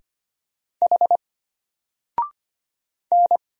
Morse code, heard as a keyed tone, about 25 wpm, 700 Hz. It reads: H N
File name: Single-Letter-Number-25wpm